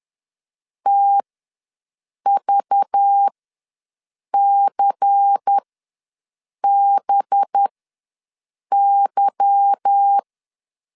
Morse code, heard as a keyed tone, about 10 wpm, spaced as for 6 wpm, 800 Hz. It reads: TVCBY